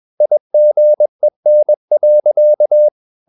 Morse code, heard as a keyed tone, about 21 wpm, 600 Hz. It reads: IGEN.
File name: Ferie QRS 21wpm